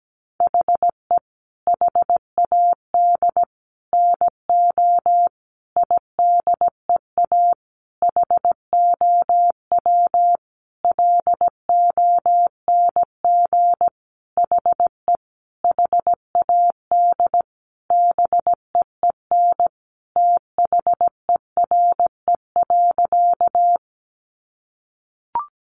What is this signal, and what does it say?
Morse code, heard as a keyed tone, about 17 wpm, 700 Hz. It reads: HE HAD NO IDEA HOW LONG HE HAD BEEN THERE.
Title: Sentences from Top 300 Words 17wpm